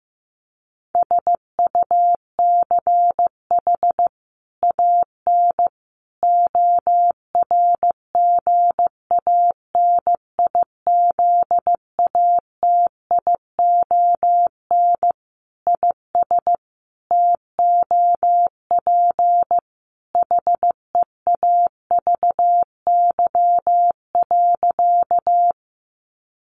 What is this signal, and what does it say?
Morse code, heard as a keyed tone, about 15 wpm, 700 Hz. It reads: SUCH AN ORGANIZATION IS TOP HEAVY.
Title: Sentences from Top 1100 Words 15wpm